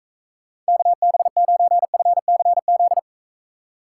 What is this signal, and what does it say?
Morse code, heard as a keyed tone, about 42 wpm, 700 Hz. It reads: KB9FCZ